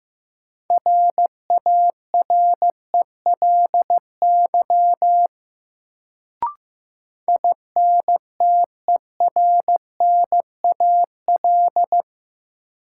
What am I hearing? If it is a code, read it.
Morse code, heard as a keyed tone, about 16 wpm, 700 Hz. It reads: RARELY INTERNAL
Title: Sentences from Top 1400 Words - New Word Drill - 15wpm